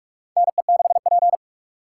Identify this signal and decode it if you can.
Morse code, heard as a keyed tone, about 44 wpm, 700 Hz. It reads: NE6P
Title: USA Call Signs - 4 Characters - 2x Character Spacing - 45wpm